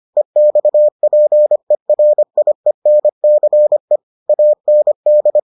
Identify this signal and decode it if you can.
Morse code, heard as a keyed tone, about 25 wpm, 600 Hz. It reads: EXPERIENCE AND